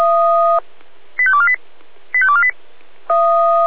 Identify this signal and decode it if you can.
Five-tone selective call (ZVEI), heard as quick runs of five tones, tones of about 70 ms. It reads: 85248 85248